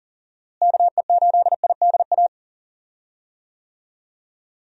Morse code, heard as a keyed tone, about 40 wpm, 700 Hz. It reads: KE8IDA
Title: USA Call Signs - 6 Characters - Repeated 3x - 40wpm